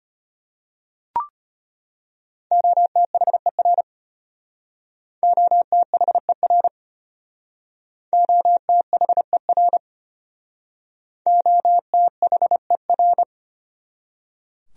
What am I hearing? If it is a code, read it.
Morse code, heard as a keyed tone, about 33 wpm, 700 Hz. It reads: OTHER OTHER OTHER OTHER